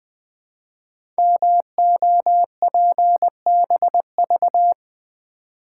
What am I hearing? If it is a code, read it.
Morse code, heard as a keyed tone, about 20 wpm, 700 Hz. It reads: MOPBV
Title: Code Groups - 5 Letters - 20wpm